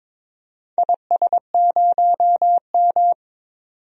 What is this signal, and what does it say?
Morse code, heard as a keyed tone, about 22 wpm, 700 Hz. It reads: IS0M
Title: DXCC-100 Call Signs - 4 Characters - 22wpm